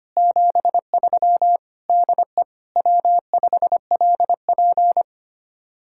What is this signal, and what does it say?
Morse code, heard as a keyed tone, about 33 wpm, 700 Hz. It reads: 73 DE W5LP